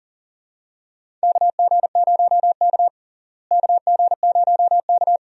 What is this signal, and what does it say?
Morse code, heard as a keyed tone, about 40 wpm, 700 Hz. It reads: KG0K KG0K
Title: USA Call Signs - 4 Characters - Repeated 2x - 40wpm